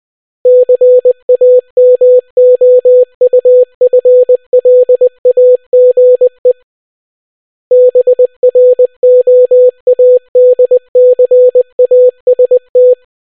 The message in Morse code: CAMOUFLAGE BROADCAST